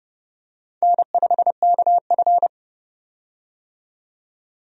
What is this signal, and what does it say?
Morse code, heard as a keyed tone, about 29 wpm, 700 Hz. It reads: N5KF
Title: USA Call Signs - 4 Characters - 30wpm